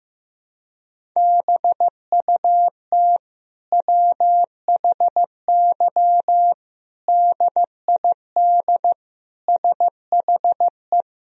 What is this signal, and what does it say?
Morse code, heard as a keyed tone, about 15 wpm, 700 Hz. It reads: BUT WHY DID SHE
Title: Sentences from Top 1000 Words - Review Part 1 - 15wpm